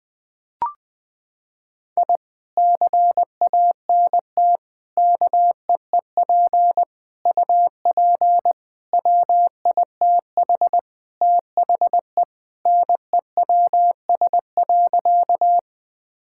Morse code, heard as a keyed tone, about 21 wpm, 700 Hz. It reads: I CANT KEEP UP WITH THE NEWS.
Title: Sentences with 2-Word Phrases - 20wpm